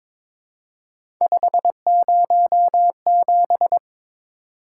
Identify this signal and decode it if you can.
Morse code, heard as a keyed tone, about 22 wpm, 700 Hz. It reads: 507
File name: practice / 22wpm / Three Letter-Number Combinations - 22wpm